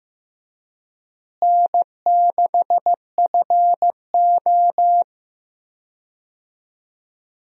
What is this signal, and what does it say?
Morse code, heard as a keyed tone, about 15 wpm, 700 Hz. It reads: N6FO
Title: USA Call Signs - 4 Characters - Repeated 2x - 15wpm